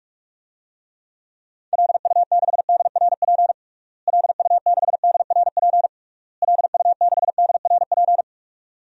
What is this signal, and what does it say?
Morse code, heard as a keyed tone, about 46 wpm, 700 Hz. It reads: LU6DRP LU6DRP LU6DRP
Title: DXCC-100 Call Signs - 6 Characters - Repeated 3x - 45wpm